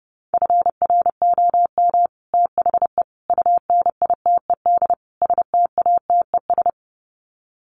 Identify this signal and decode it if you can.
Morse code, heard as a keyed tone, about 30 wpm, 700 Hz. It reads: FROM THE UNITED STATES
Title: Sets of 4 Words 30wpm